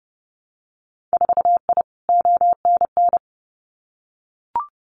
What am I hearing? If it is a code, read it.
Morse code, heard as a keyed tone, about 30 wpm, 700 Hz. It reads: 4I ONN
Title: Winter Field Day Exchanges Encore 30wpm